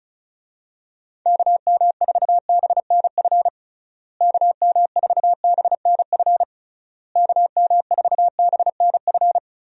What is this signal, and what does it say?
Morse code, heard as a keyed tone, about 34 wpm, 700 Hz. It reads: KM4BNF KM4BNF KM4BNF
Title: USA Call Signs - 6 Characters - Repeated 3x - 35wpm